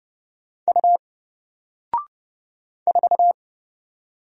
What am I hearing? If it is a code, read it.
Morse code, heard as a keyed tone, about 30 wpm, 700 Hz. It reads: U 4
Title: Single Character (Letter:Number:Punctuation:Prosign) Alt 30wpm